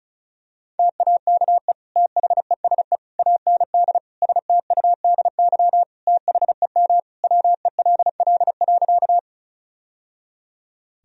Morse code, heard as a keyed tone, about 36 wpm, 700 Hz. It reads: TAKE THESE AND STUDY THEM WELL.
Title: Sentences from Top 1000 Words - Review Part 1 - 35wpm